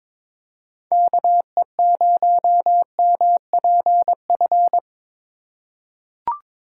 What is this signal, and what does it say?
Morse code, heard as a keyed tone, about 21 wpm, 700 Hz. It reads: KE0MPF
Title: USA Call Signs - 6 Characters - 3x Character Spacing - 22wpm